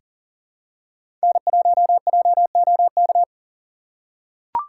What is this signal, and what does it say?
Morse code, heard as a keyed tone, about 40 wpm, 700 Hz. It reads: N1JOK
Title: USA Call Signs - 5 Characters - 2x Character Spacing - 40wpm